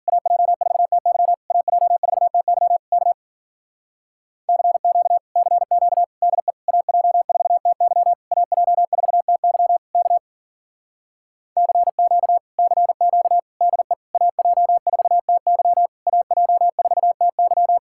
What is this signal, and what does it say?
Morse code, heard as a keyed tone, about 49 wpm, 700 Hz. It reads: AJ4TY AJ4TY K CQ CQ DE AJ4TY AJ4TY K CQ CQ DE AJ4TY AJ4TY
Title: Speed-Racing Common QSO Phrases - 40wpm